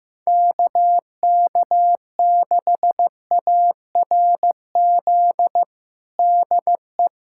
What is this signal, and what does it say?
Morse code, heard as a keyed tone, about 16 wpm, 700 Hz. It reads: KK6ARZ DE
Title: USA Callsign DE Callsign 15wpm